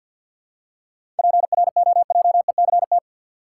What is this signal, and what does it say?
Morse code, heard as a keyed tone, about 53 wpm, 700 Hz. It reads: PROJECT
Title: Seven Letter Words - 50wpm